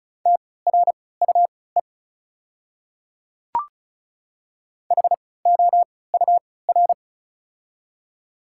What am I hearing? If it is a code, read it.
Morse code, heard as a keyed tone, about 34 wpm, spaced as for 20 wpm, 690 Hz. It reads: TRUE HOUR
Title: 3x Character Spacing - Top 500 Words - 35wpm